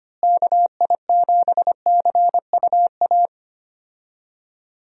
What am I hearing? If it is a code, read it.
Morse code, heard as a keyed tone, about 25 wpm, 700 Hz. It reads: KI7CUA
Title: USA Call Signs - 6 Characters - Repeated 2x - 25wpm